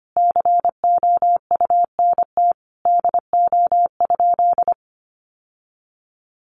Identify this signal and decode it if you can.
Morse code, heard as a keyed tone, about 25 wpm, 700 Hz. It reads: COUNT DO?
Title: Sentences from Top 1500 Words 25wpm